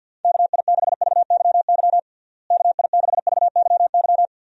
Morse code, heard as a keyed tone, about 52 wpm, 700 Hz. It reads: KI6VYY KI6VYY
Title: USA Call Signs - 6 Characters - Repeated 3x - 50wpm